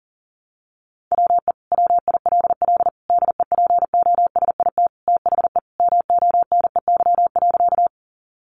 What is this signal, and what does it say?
Morse code, heard as a keyed tone, about 40 wpm, 700 Hz. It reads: WE WILL DEPOSIT THE MONEY.
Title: Sentences from Top 1300 Words 40wpm